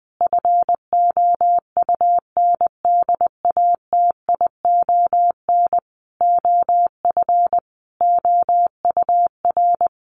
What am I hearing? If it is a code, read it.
Morse code, heard as a keyed tone, about 20 wpm, 700 Hz. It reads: FOUNDATION OF OUR
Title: Sentences from Top 1200 Words 20wpm